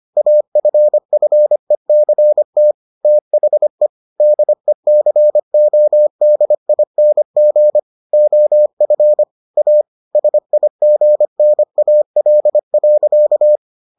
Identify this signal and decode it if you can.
Morse code, heard as a keyed tone, about 25 wpm, 600 Hz. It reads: AFFECT THE DECODING OF A SIGNAL.